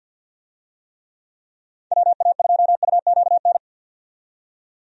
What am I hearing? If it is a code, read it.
Morse code, heard as a keyed tone, about 53 wpm, 700 Hz. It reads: WA2UQN
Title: USA Call Signs - 6 Characters - 50wpm